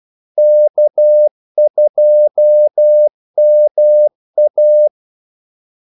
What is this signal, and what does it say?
Morse code, heard as a keyed tone, about 12 wpm, 600 Hz. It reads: K2MA
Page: 2 ekstra ordmellemrum mellem kaldesignalerne